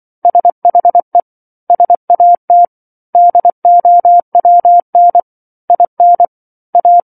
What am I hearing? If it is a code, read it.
Morse code, heard as a keyed tone, about 24 wpm, 700 Hz. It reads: SHE SAT DOWN IN A
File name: 07-Chapter-07 24 WPM